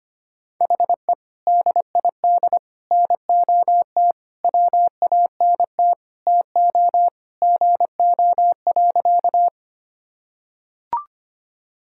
Morse code, heard as a keyed tone, about 25 wpm, 700 Hz. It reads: HE DID NOT WANT TO GO.
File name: practice / 25wpm / Sentences with 3-Word Phrases - 25wpm